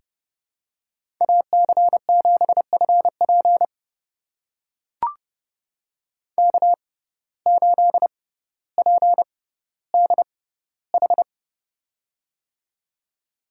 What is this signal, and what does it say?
Morse code, heard as a keyed tone, about 30 wpm, spaced as for 15 wpm, 700 Hz. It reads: AC7FP K8PDH
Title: USA Call Signs - 5 Characters - 6x Character Spacing - 30wpm